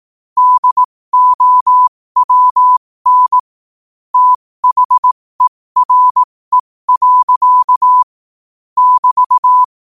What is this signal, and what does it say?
Morse code, heard as a keyed tone, about 18 wpm, spaced as for 15 wpm, 1000 Hz. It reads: DOWN THERE. <BT>
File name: qotdfiles / 15wpm